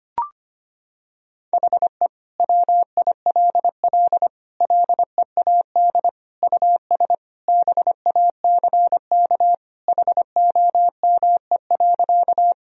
Morse code, heard as a keyed tone, about 25 wpm, 700 Hz. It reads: HE WILL LEAD US BACK HOME.
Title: Sentences from Top 1000 Words - Review Part 2 - 25wpm